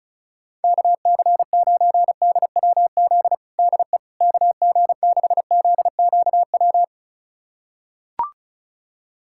Morse code, heard as a keyed tone, about 36 wpm, 700 Hz. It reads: KC9DWZ DE KG6ZQW
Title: USA Callsign DE Callsign - 6 Characters - 35wpm